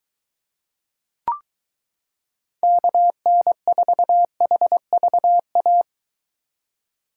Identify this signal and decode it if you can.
Morse code, heard as a keyed tone, about 23 wpm, 700 Hz. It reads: KN4HVA